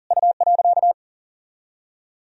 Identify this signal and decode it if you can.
Morse code, heard as a keyed tone, about 48 wpm, 700 Hz. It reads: U.
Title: Speed-Racing - Sets of 3 Words 40wpm